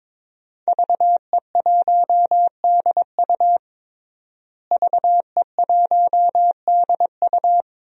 Morse code, heard as a keyed tone, about 22 wpm, 700 Hz. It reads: VE1DU VE1DU
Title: DXCC-100 Call Signs - 5 Characters - Repeated 2x - 22wpm